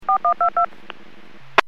DTMF, heard as telephone touch tones, tones of about 100 ms. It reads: 4122